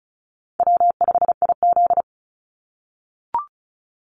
Morse code, heard as a keyed tone, about 36 wpm, 700 Hz. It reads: W5IZ
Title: Call Signs - 35wpm